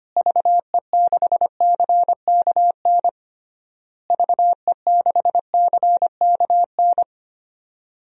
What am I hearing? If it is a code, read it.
Morse code, heard as a keyed tone, about 25 wpm, 700 Hz. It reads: VE6CKN VE6CKN
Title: DXCC-100 Call Signs - 6 Characters - Repeated 3x - 25wpm